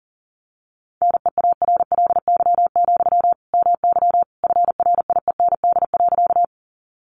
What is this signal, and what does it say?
Morse code, heard as a keyed tone, about 40 wpm, 700 Hz. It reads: NEARLY, MY FRIEND.
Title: Sentences from Top 1300 Words 40wpm